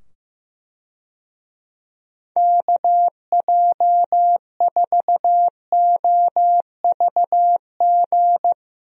Morse code, heard as a keyed tone, about 15 wpm, 700 Hz. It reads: KJ4OVG
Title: USA Call Signs - 6 Characters - 15wpm